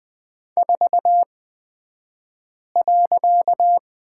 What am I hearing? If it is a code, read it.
Morse code, heard as a keyed tone, about 20 wpm, spaced as for 5 wpm, 700 Hz. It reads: 4.
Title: 254 Lesson 41d - Period - Two Characters 20wpm